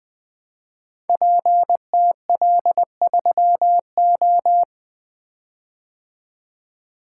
Morse code, heard as a keyed tone, about 20 wpm, 700 Hz. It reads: PTL3O
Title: Code Groups - 5 Letter:Numbers - 20wpm